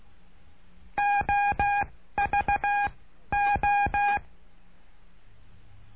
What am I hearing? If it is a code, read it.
Morse code, heard as a keyed tone, about 16 wpm, 810 Hz. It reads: O V O
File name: cursos / CURSO DE TELEGRAFIA / Telegrafia - Trilha 09